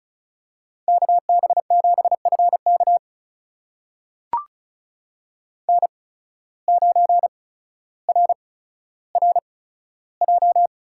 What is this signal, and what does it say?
Morse code, heard as a keyed tone, about 36 wpm, spaced as for 15 wpm, 700 Hz. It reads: KB7FK N9RRJ
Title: USA Call Signs - 5 Characters - 8x Character Spacing - 35wpm